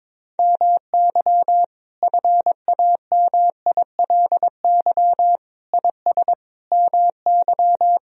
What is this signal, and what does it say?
Morse code, heard as a keyed tone, about 22 wpm, 700 Hz. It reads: MY FAMILY IS MY
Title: Sentences from Top 1000 Words - Review Part 4 - 22wpm